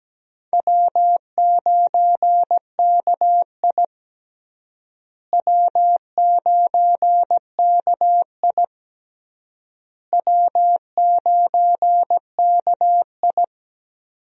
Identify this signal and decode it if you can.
Morse code, heard as a keyed tone, about 17 wpm, 700 Hz. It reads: W9KI W9KI W9KI
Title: USA Call Signs - 4 Characters - Repeated 3x - 17wpm